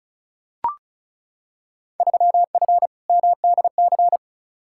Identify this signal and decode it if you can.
Morse code, heard as a keyed tone, about 36 wpm, 690 Hz. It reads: 3F MDC
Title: Field Day Exchanges Encore - 35wpm